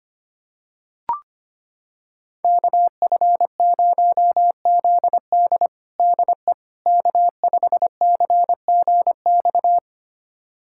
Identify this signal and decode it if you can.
Morse code, heard as a keyed tone, about 25 wpm, 700 Hz. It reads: KF0ZD DE K5CGX
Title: USA Callsign DE Callsign - 5 Characters - 25wpm